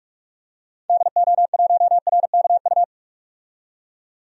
Morse code, heard as a keyed tone, about 46 wpm, 700 Hz. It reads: DO1RKU
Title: DXCC-100 Call Signs - 6 Characters - 45wpm